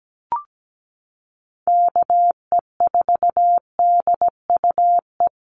Morse code, heard as a keyed tone, about 17 wpm, 700 Hz. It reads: KE4DUE
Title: USA Call Signs Repeated 2x 17wpm